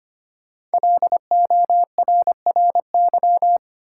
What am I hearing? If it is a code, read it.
Morse code, heard as a keyed tone, about 25 wpm, 700 Hz. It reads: LORRY
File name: 5 Letter Words - First 3 Letters Given - 25wpm